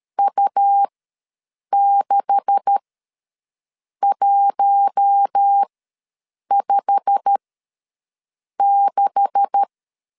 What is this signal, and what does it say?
Morse code, heard as a keyed tone, about 13 wpm, spaced as for 6 wpm, 800 Hz. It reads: U6156
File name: LEZIONI / 9 286 ws 12wpm 3351